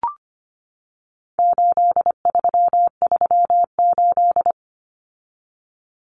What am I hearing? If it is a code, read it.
Morse code, heard as a keyed tone, about 25 wpm, 700 Hz. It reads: 8338
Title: Numbers - 4-Digit 25wpm